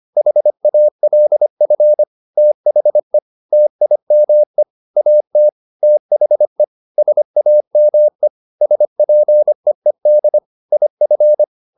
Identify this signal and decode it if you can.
Morse code, heard as a keyed tone, about 25 wpm, 600 Hz. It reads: HALF THE TIME AT THE SAME SPEED IF
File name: SS220 25